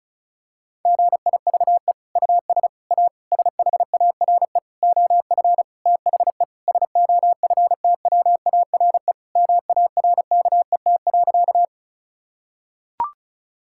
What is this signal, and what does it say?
Morse code, heard as a keyed tone, about 34 wpm, 700 Hz. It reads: GIVE US A SHARE OF THE SOFTWARE MARKET.
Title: Sentences from Top 1000 Words - Review Part 6 - 35wpm